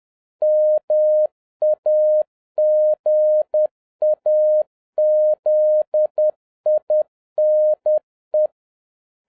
Morse code, heard as a keyed tone, about 10 wpm, 610 Hz. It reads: MAGAZINE
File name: 2011-07-mp3 / n21 GREETINGS FROM SINGAPORE 10wpm